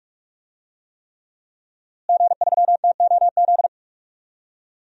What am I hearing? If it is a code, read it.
Morse code, heard as a keyed tone, about 44 wpm, 700 Hz. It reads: G3TOZ
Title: DXCC-100 Call Signs - 5 Characters - Repeated 3x - 45wpm